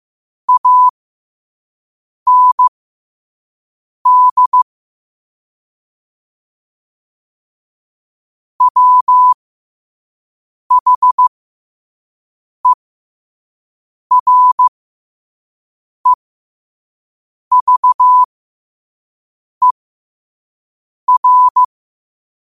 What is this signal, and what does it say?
Morse code, heard as a keyed tone, about 15 wpm, spaced as for 5 wpm, 1000 Hz. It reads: AND WHEREVER
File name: qotdfiles / 5wpm